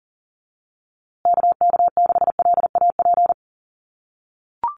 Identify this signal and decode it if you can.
Morse code, heard as a keyed tone, about 40 wpm, 700 Hz. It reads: KK6LAP
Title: USA Call Signs Repeated 2x 40wpm